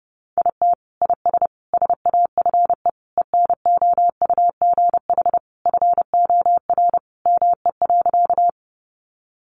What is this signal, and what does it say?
Morse code, heard as a keyed tone, about 30 wpm, 700 Hz. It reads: IT IS SAFE ENOUGH FOR ME.
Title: Sentences from Top 900 Words 30wpm